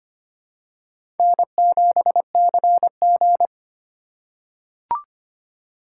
Morse code, heard as a keyed tone, about 25 wpm, 700 Hz. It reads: N7CG
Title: USA Call Signs - 4 Characters - 6x Character Spacing - 25wpm